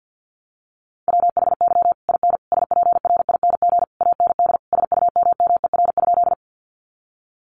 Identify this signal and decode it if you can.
Morse code, heard as a keyed tone, about 50 wpm, 700 Hz. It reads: WHY IN SPRING AND SUMMER?
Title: Sentences from Top 1200 Words 50wpm